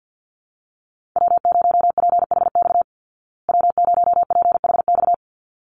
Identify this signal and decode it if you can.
Morse code, heard as a keyed tone, about 48 wpm, 700 Hz. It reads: W0PHX W0PHX
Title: Call Signs Repeated 3x - 50wpm